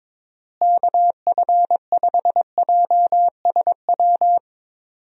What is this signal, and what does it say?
Morse code, heard as a keyed tone, about 22 wpm, 700 Hz. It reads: KF5JSW
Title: USA Call Signs - 6 Characters - Repeated 3x - 22wpm